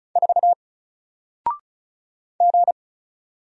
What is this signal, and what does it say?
Morse code, heard as a keyed tone, about 35 wpm, 700 Hz. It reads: 4 G
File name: Single-Letter-Number-35wpm